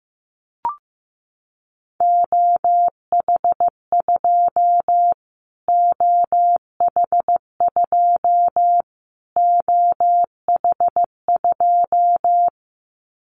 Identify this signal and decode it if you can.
Morse code, heard as a keyed tone, about 15 wpm, 700 Hz. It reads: OH2 OH2 OH2
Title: Callsign Prefixes for Top 25 DXCC - Repeated 3x - 15wpm